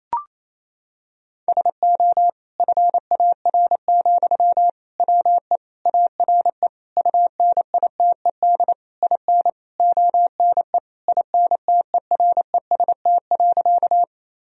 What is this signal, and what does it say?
Morse code, heard as a keyed tone, about 28 wpm, 700 Hz. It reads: SO FAR, WE ARE UNITED IN ONE INTEREST.
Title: Sentences from Top 1000 Words - Review Part 4 - 28wpm